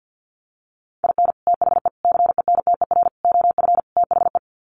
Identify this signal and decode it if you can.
Morse code, heard as a keyed tone, about 48 wpm, 700 Hz. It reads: IN THE CENTER OF THE
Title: Sets of 5 Words 50wpm